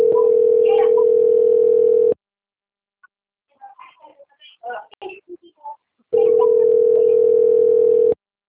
Ringback tone (North America), 440 + 480 Hz.